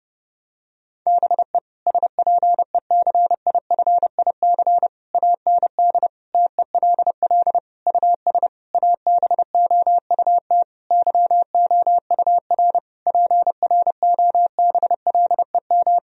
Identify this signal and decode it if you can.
Morse code, heard as a keyed tone, about 30 wpm, 700 Hz. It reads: BE SPECIFIC AND TELL US ABOUT YOUR PROBLEM
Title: Sentences from Top 1000 Words - Review Part 3 - 30wpm